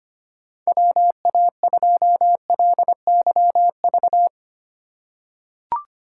Morse code, heard as a keyed tone, about 25 wpm, 700 Hz. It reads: WA2LYV
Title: USA Call Signs - 3x Character Spacing - 25wpm